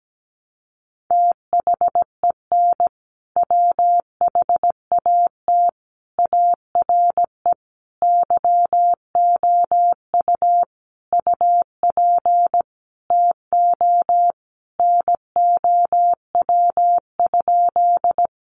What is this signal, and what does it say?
Morse code, heard as a keyed tone, about 17 wpm, 700 Hz. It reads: THEN WHAT ARE YOU UP TO NOW?
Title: Sentences from Top 100 Words 17wpm